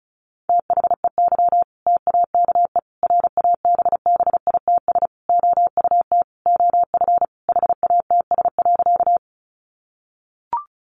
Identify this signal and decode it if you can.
Morse code, heard as a keyed tone, about 34 wpm, 700 Hz. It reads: THEY TAKE RABBITS OUT OF HATS.
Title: Easy Phrases 35wpm